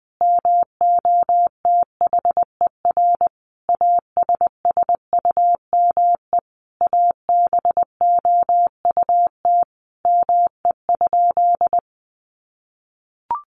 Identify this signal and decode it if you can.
Morse code, heard as a keyed tone, about 20 wpm, 700 Hz. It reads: MOTHER ASSUME ABOUT ME?
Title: Sentences from Top 1500 Words 20wpm